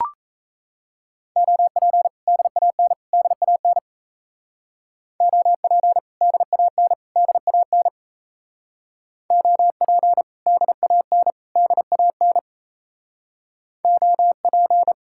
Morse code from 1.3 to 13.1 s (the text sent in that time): OP DAN DAN OP DAN DAN OP DAN DAN